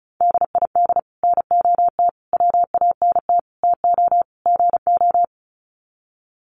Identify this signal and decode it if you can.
Morse code, heard as a keyed tone, about 34 wpm, 700 Hz. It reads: DID NOT WANT TO GO
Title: Sets of 5 Words 35wpm